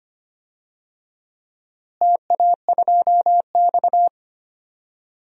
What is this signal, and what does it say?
Morse code, heard as a keyed tone, about 25 wpm, 700 Hz. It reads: TA2X
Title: DXCC-100 Call Signs - 4 Characters - Repeated 3x - 25wpm